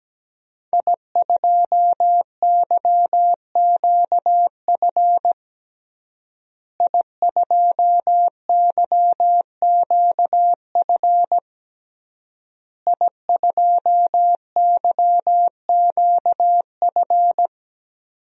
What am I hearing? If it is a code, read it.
Morse code, heard as a keyed tone, about 17 wpm, 700 Hz. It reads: I2YQF I2YQF I2YQF
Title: DXCC-100 Call Signs - 5 Characters - Repeated 3x - 17wpm